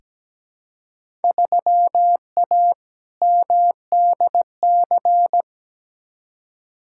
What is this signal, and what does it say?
Morse code, heard as a keyed tone, about 17 wpm, 700 Hz. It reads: 3A MDC
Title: Field Day Exchanges - 17wpm